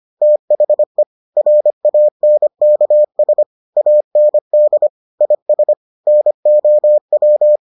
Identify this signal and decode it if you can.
Morse code, heard as a keyed tone, about 25 wpm, 600 Hz. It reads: THE RANKS AND IS NOW